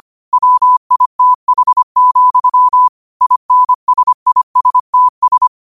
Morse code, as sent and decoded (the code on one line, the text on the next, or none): .-- .. - .... --..-- / .. -. ... .. ... - ...
WITH, INSISTS